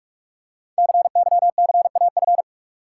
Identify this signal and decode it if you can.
Morse code, heard as a keyed tone, about 44 wpm, 700 Hz. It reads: CYCAF
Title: Code Groups - 5 Letters - 45wpm